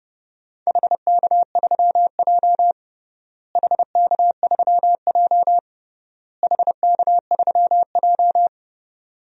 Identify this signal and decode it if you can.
Morse code, heard as a keyed tone, about 30 wpm, 700 Hz. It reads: HK3J HK3J HK3J
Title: DXCC-100 Call Signs - 4 Characters - Repeated 3x - 30wpm